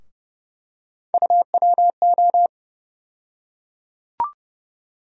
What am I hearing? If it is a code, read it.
Morse code, heard as a keyed tone, about 30 wpm, 700 Hz. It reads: UWO
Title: USA Call Sign Postfixes - 3 Letters 30wpm